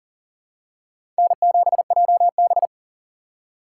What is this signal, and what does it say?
Morse code, heard as a keyed tone, about 40 wpm, 700 Hz. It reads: N7JB
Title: USA Call Signs - 4 Characters - 3x Character Spacing - 40wpm